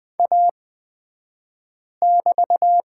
Morse code, heard as a keyed tone, about 20 wpm, spaced as for 5 wpm, 700 Hz. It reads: A<BT>
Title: 223 Lesson 36d - BT - Two Characters 20wpm